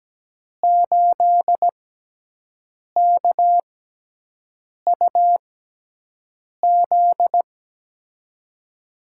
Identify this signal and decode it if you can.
Morse code, heard as a keyed tone, about 17 wpm, spaced as for 6 wpm, 700 Hz. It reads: 8KUZ
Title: USA Call Signs - 6 Characters - 6x Character Spacing - 17wpm